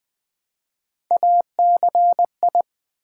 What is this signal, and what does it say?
Morse code, heard as a keyed tone, about 20 wpm, 700 Hz. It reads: ACI
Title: Build A Word - 1 Letter At A Time - 3 to 5-Letter Words - 20wpm